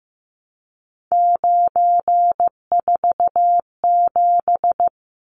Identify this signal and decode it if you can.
Morse code, heard as a keyed tone, about 15 wpm, 700 Hz. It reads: 947
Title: Numbers - 3-Digit 15wpm